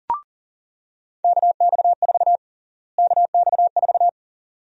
Morse code, heard as a keyed tone, about 40 wpm, 700 Hz. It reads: KX4 KX4
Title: USA Call Sign Prefixes - Repeated 2x - 40wpm